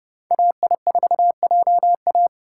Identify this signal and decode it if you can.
Morse code, heard as a keyed tone, about 28 wpm, 700 Hz. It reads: AI4JA